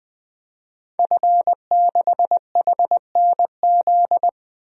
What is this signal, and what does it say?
Morse code, heard as a keyed tone, about 20 wpm, 700 Hz. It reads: F6HNZ
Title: DXCC-100 Call Signs - 5 Characters - 20wpm